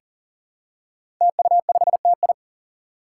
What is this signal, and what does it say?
Morse code, heard as a keyed tone, about 40 wpm, 700 Hz. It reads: TU5TI
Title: Five Challenging Letter-Number Combinations 40wpm